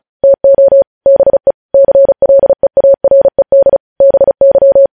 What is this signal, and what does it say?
Morse code, heard as a keyed tone, about 36 wpm, 550 Hz. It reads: TO BE CLEARED BY
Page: News Headlines in Morse Code at 35 WPM